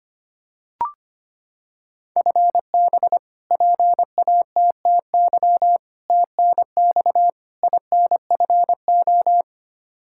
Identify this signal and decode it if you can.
Morse code, heard as a keyed tone, about 25 wpm, 700 Hz. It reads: FB PATTY TNX INFO
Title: K1USN SST QSO Practice 25wpm